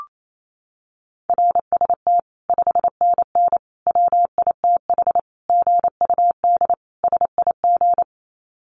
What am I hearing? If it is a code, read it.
Morse code, heard as a keyed tone, about 28 wpm, 700 Hz. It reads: RST 5NN WITH GUD SIG
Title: Common QSO Phrases 28wpm